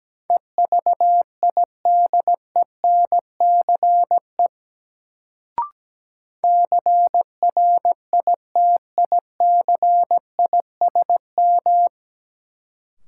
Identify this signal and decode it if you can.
Morse code, heard as a keyed tone, about 17 wpm, 700 Hz. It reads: EVIDENCE CRITICISM
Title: Sentences from Top 1300 Words - New Word Drill - 17wpm